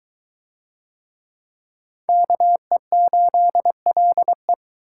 Morse code, heard as a keyed tone, about 22 wpm, 700 Hz. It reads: KE8LE